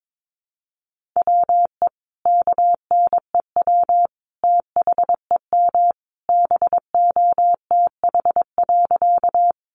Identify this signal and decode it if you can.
Morse code, heard as a keyed tone, about 22 wpm, 700 Hz. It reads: WE KNEW THEM BOTH.
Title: Sentences from Top 400 Words 22wpm